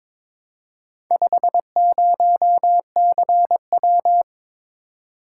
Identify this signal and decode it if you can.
Morse code, heard as a keyed tone, about 22 wpm, 700 Hz. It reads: 50CW
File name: practice / 22wpm / Four Letter-Number Combinations - 22wpm